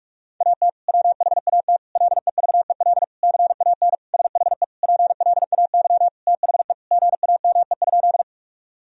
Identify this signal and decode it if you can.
Morse code, heard as a keyed tone, about 46 wpm, 700 Hz. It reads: AT WHAT LEVEL CAN SHE PLAY THE GAME?
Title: Sentences from Top 1000 Words - Review Part 4 - 45wpm